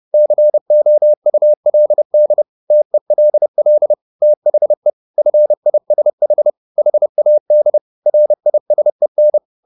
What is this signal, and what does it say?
Morse code, heard as a keyed tone, about 30 wpm, 600 Hz. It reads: COULD TELL THE FISH HAD RISEN